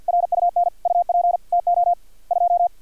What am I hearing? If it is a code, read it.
Morse code, heard as a keyed tone, about 50 wpm, 700 Hz. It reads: RUN UP TO 2
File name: archived / CRSnet